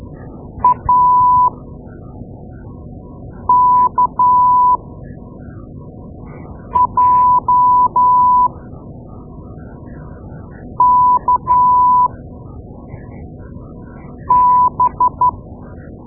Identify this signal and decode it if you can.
Morse code, heard as a keyed tone, about 12 wpm, spaced as for 4 wpm, 990 Hz. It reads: AKJKB
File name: Morse Lessons / Morse Lessons / Tape-2A